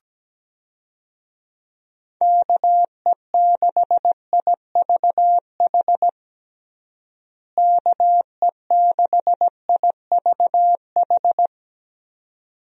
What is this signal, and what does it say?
Morse code, heard as a keyed tone, about 17 wpm, 700 Hz. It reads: KE6IVH KE6IVH
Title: USA Call Signs - 6 Characters - Repeated 2x - 17wpm